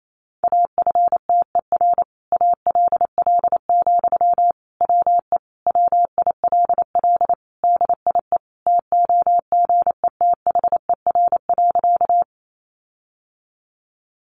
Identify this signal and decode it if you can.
Morse code, heard as a keyed tone, about 28 wpm, 700 Hz. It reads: AFTER ALL, WE WILL DIE TOGETHER.
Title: Sentences from Top 1300 Words 28wpm